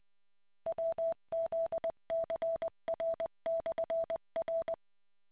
Morse code, heard as a keyed tone, about 20 wpm, 660 Hz. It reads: WZCR/R